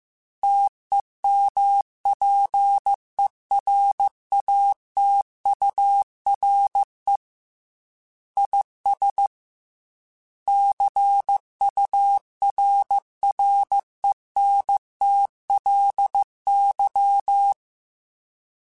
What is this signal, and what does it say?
Morse code, heard as a keyed tone, about 15 wpm, 800 Hz. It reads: TEMPERATURE IS CURRENTLY
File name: Weather Sets / Weather Sets-mp3s / Weather 4 28 @13